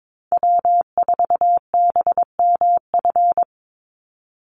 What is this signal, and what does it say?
Morse code, heard as a keyed tone, about 22 wpm, 700 Hz. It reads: W4BMF
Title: USA Call Signs 22wpm